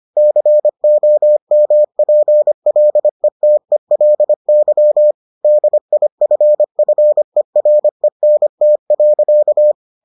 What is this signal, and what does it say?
Morse code, heard as a keyed tone, about 25 wpm, 600 Hz. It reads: COMPLETELY DIFFERENT.